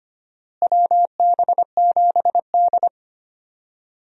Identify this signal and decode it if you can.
Morse code, heard as a keyed tone, about 25 wpm, 700 Hz. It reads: WB7D
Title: USA Call Signs - 4 Characters - Repeated 3x - 25wpm